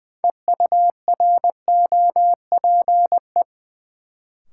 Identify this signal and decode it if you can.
Morse code, heard as a keyed tone, about 21 wpm, 700 Hz. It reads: EUROPE
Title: Sets of 5 Words Encore - New Word Drill - 20wpm